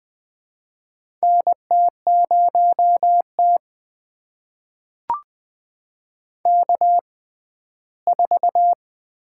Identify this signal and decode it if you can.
Morse code, heard as a keyed tone, about 20 wpm, spaced as for 14 wpm, 700 Hz. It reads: NT0T K4
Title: USA Call Signs - 6x Character Spacing - 20wpm